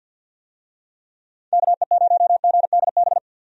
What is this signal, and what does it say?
Morse code, heard as a keyed tone, about 50 wpm, 700 Hz. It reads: KE0GDB
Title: USA Call Signs - 6 Characters - 50wpm